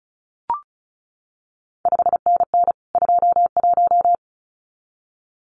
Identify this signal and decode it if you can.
Morse code, heard as a keyed tone, about 36 wpm, 700 Hz. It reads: 5NN 21
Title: CQ WW DX Contest Exchange Encore - 35wpm